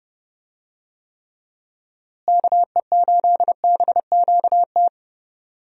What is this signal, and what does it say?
Morse code, heard as a keyed tone, about 32 wpm, 700 Hz. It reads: KE8BQT
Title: USA Call Signs - 6 Characters - Repeated 2x - 30wpm